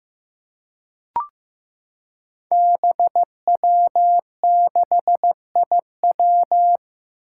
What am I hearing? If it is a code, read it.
Morse code, heard as a keyed tone, about 15 wpm, 700 Hz. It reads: BW6IW
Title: Five Challenging Letter-Number Combinations 15wpm